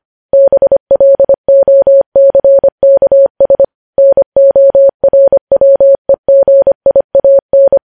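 Morse code, heard as a keyed tone, about 25 wpm, 550 Hz. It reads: BLOCKS NORWEGIAN
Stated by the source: News headlines sent at 25.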